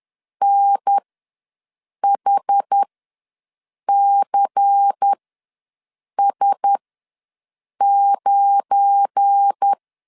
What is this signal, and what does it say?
Morse code, heard as a keyed tone, about 10 wpm, spaced as for 6 wpm, 800 Hz. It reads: NHCS9
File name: LEZIONI / 10 SIGNS / 10A SLASH ws 10wpm 4121